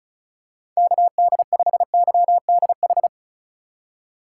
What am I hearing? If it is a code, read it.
Morse code, heard as a keyed tone, about 36 wpm, 700 Hz. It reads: KD5YDH
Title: USA Call Signs - 8x Character Spacing - 35wpm